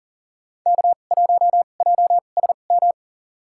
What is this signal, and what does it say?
Morse code, heard as a keyed tone, about 40 wpm, spaced as for 29 wpm, 700 Hz. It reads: K1JSM
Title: USA Call Signs - 5 Characters - 2x Character Spacing - 40wpm